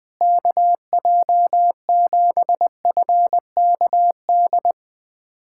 Morse code, heard as a keyed tone, about 20 wpm, 700 Hz. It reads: KJ7FKD
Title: USA Callsign DE Callsign 20wpm